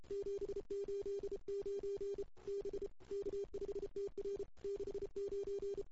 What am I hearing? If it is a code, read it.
Morse code, heard as a keyed tone, about 31 wpm, 400 Hz. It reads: 789 B K5TR 69
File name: CW / 2008-ss-cw-k5tr-op-n5rz -05:00